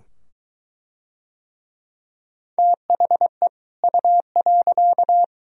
Morse code, heard as a keyed tone, about 20 wpm, 700 Hz. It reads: THE U.